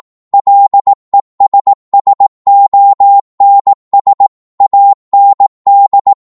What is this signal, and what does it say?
Morse code, heard as a keyed tone, about 18 wpm, 800 Hz. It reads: LESSONS AND